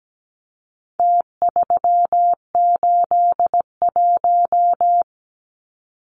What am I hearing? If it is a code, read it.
Morse code, heard as a keyed tone, about 17 wpm, 700 Hz. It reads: T381
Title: Numbers - Contest Style - 4-Digit 17wpm